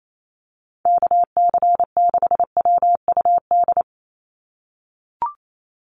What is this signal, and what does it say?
Morse code, heard as a keyed tone, about 28 wpm, 700 Hz. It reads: KC6WUD
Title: USA Call Signs 28wpm